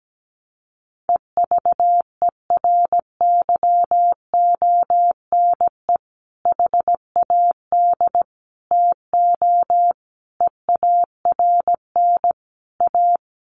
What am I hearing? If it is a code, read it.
Morse code, heard as a keyed tone, about 17 wpm, 700 Hz. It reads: EVERYONE HAD TO EARN A
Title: Sentences from Top 1500 Words 17wpm